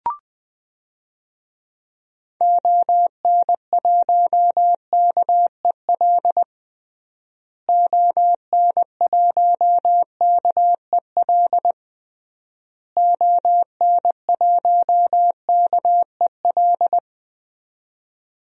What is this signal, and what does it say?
Morse code, heard as a keyed tone, about 20 wpm, 700 Hz. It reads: ON1KEL ON1KEL ON1KEL
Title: DXCC-100 Call Signs - 6 Characters - Repeated 3x - 20wpm